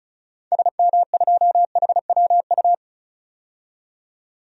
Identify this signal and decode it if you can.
Morse code, heard as a keyed tone, about 34 wpm, 700 Hz. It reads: SM2HWU
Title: DXCC-100 Call Signs - 6 Characters - 35wpm